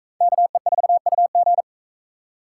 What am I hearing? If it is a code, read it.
Morse code, heard as a keyed tone, about 38 wpm, 700 Hz. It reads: KE4UG